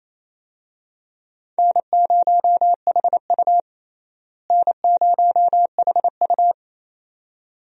Morse code, heard as a keyed tone, about 28 wpm, 700 Hz. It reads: N0HU N0HU
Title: USA Call Signs - 4 Characters - Repeated 2x - 28wpm